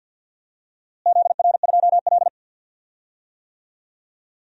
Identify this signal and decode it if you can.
Morse code, heard as a keyed tone, about 46 wpm, 700 Hz. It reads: ZR2L
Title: DXCC-100 Call Signs - 4 Characters - Repeated 2x - 50wpm